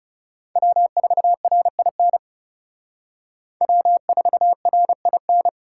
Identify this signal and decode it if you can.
Morse code, heard as a keyed tone, about 36 wpm, 700 Hz. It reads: W4RIN W4RIN
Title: Speed Racing - USA 1x3 Call Signs 30wpm